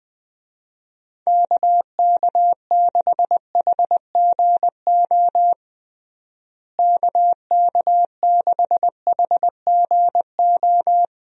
Morse code, heard as a keyed tone, about 20 wpm, 700 Hz. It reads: KK6HGO KK6HGO
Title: USA Call Signs - 6 Characters - Repeated 2x - 20wpm